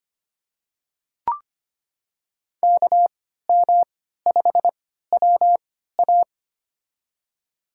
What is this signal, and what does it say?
Morse code, heard as a keyed tone, about 25 wpm, spaced as for 14 wpm, 700 Hz. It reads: KM5WA
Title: USA Call Signs - 5 Characters - 3x Character Spacing - 25wpm